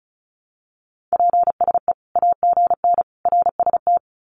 Morse code, heard as a keyed tone, about 34 wpm, 700 Hz. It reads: PSE AGN RST
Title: Common QSO Phrases 35wpm